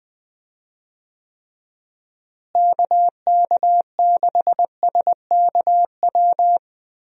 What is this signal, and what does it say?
Morse code, heard as a keyed tone, about 21 wpm, 700 Hz. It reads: KK6SKW
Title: USA Call Signs - 6 Characters - Repeated 3x - 20wpm